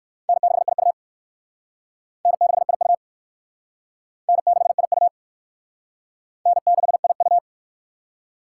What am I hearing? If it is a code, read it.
Morse code, heard as a keyed tone, about 56 wpm, 700 Hz. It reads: N6IU N6IU N6IU N6IU
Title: Speed Racing - USA 1x2 Call Signs 45wpm